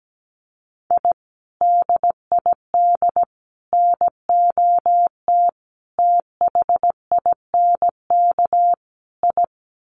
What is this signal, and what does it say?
Morse code, heard as a keyed tone, about 17 wpm, 700 Hz. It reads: I DID NOT THINK I
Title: Sets of 5 Words Encore 17wpm